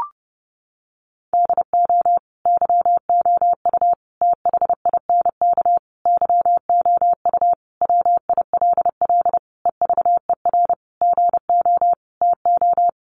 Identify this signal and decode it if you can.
Morse code, heard as a keyed tone, about 30 wpm, 700 Hz. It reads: DO YOU THINK YOU WILL EVER GO TO
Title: Sentences from Top 1300 Words 30wpm